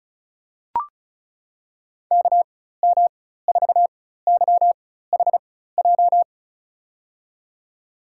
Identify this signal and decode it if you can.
Morse code, heard as a keyed tone, about 36 wpm, spaced as for 16 wpm, 700 Hz. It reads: KM4YHJ
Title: USA Call Signs - 6 Characters - 4x Character Spacing - 35wpm